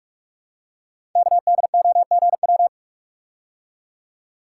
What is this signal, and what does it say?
Morse code, heard as a keyed tone, about 44 wpm, 700 Hz. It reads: KDOGW
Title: Code Groups - 5 Letters - 45wpm